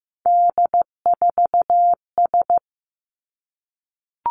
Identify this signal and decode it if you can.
Morse code, heard as a keyed tone, about 15 wpm, 700 Hz. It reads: D4S
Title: Three Challenging Letter-Number Combinations 15wpm